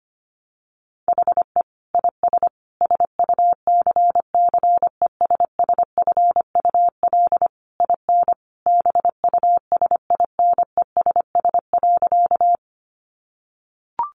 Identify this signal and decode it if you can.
Morse code, heard as a keyed tone, about 25 wpm, 700 Hz. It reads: HE IS SUCCESSFUL IN BUSINESS.
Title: Sentences from Top 900 Words 25wpm